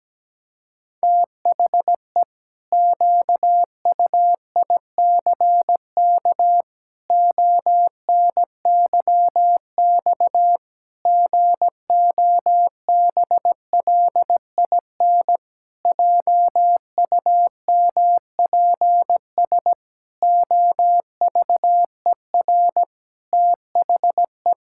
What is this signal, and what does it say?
Morse code, heard as a keyed tone, about 17 wpm, 700 Hz. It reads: THE QUICK ONYX GOBLIN JUMPS OVER THE
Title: Pangram Challenge v2 - 17wpm